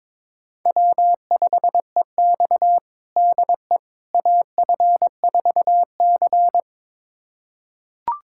Morse code, heard as a keyed tone, about 22 wpm, 700 Hz. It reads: W5EX DE AF4C
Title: USA Callsign DE Callsign - 4 Characters - 22wpm